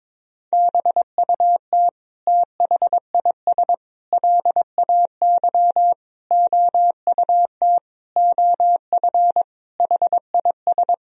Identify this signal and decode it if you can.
Morse code, heard as a keyed tone, about 22 wpm, 700 Hz. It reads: BUT THIS LAY OUT OF HIS
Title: Sentences from Top 1000 Words - Review Part 2 - 22wpm